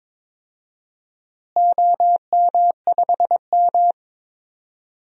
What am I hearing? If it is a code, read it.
Morse code, heard as a keyed tone, about 22 wpm, 700 Hz. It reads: OM5M
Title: DXCC-100 Call Signs - 4 Characters - Repeated 2x - 22wpm